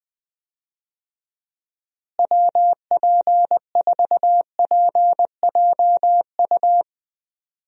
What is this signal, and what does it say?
Morse code, heard as a keyed tone, about 20 wpm, 700 Hz. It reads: WP4PJU
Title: USA Call Signs - 6 Characters - Repeated 2x - 20wpm